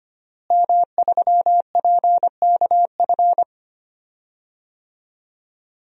Morse code, heard as a keyed tone, about 25 wpm, 700 Hz. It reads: M3PKF
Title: DXCC-100 Call Signs - 5 Characters - Repeated 3x - 25wpm